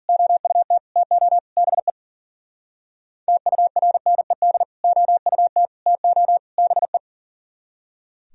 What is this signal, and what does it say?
Morse code, heard as a keyed tone, about 42 wpm, 700 Hz. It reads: OUT TO BE TURNED OUT TO BE
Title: Speed-Racing - Sets of 4 Words 40wpm